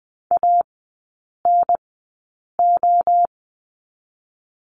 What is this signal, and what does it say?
Morse code, heard as a keyed tone, about 20 wpm, 700 Hz. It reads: A N O
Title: 023 Lesson 5e - Letter o - Three Characters 20wpm